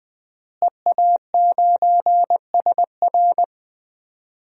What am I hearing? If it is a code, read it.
Morse code, heard as a keyed tone, about 20 wpm, 700 Hz. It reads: EA9SR
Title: DXCC-100 Call Signs - 5 Characters - Repeated 2x - 20wpm